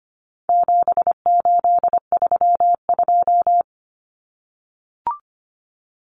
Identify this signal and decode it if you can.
Morse code, heard as a keyed tone, about 25 wpm, 700 Hz. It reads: 7832
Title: Common Cut Numbers - 4-Digit 25wpm